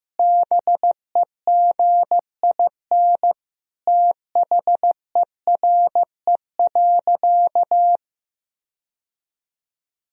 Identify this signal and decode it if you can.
Morse code, heard as a keyed tone, about 15 wpm, 700 Hz. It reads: BEGIN THERE.
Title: Sentences from Top 1000 Words - Review Part 1 - 15wpm